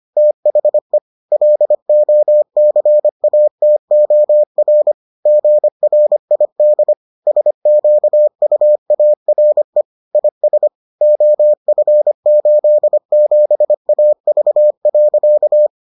Morse code, heard as a keyed tone, about 25 wpm, 600 Hz. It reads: THE LOCATOR GRID SQUARE IS OF87AV.